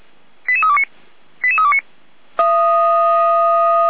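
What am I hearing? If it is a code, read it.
Five-tone selective call (ZVEI), heard as quick runs of five tones, tones of about 70 ms. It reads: 88329 88329
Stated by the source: ZVEI five-tone selective call received with an SDR, archived by the Feuerwehr Orlamünde fire brigade.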